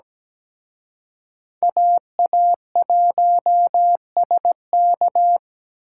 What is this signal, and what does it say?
Morse code, heard as a keyed tone, about 21 wpm, 700 Hz. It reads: AA1SK
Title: Speed Racing - USA Call Signs 17wpm